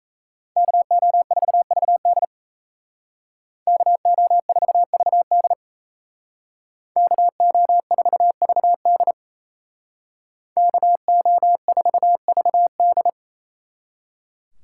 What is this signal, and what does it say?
Morse code, heard as a keyed tone, about 36 wpm, 700 Hz. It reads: KO4VD KO4VD KO4VD KO4VD